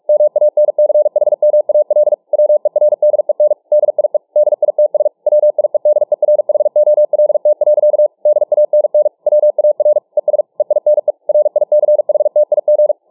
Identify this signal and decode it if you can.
Morse code, heard as a keyed tone, about 47 wpm, 600 Hz. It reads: MANCHMAL WERDEN DIE DITS WIEDERHOLT. DANN WAR ES EINE RICHTIG